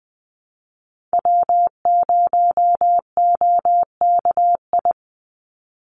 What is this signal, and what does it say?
Morse code, heard as a keyed tone, about 20 wpm, 700 Hz. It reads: W0OKI
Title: USA Call Signs 20wpm